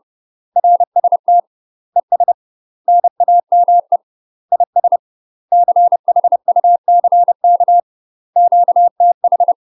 Morse code, heard as a keyed tone, about 30 wpm, 700 Hz. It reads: RST ES NAME IS CHUCK QTH
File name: UK MASTER QSOs TRAINING / 30WPM / UK MASTER QSOs 0035